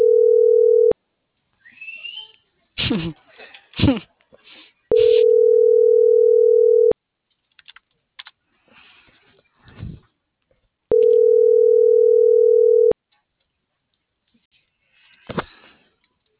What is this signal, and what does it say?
Ringback tone (North America), 440 + 480 Hz.